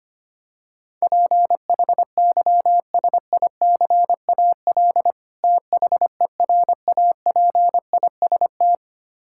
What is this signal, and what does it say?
Morse code, heard as a keyed tone, about 25 wpm, 700 Hz. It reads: PHYSICAL THERAPIST
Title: K1USN SST QSO Practice 25wpm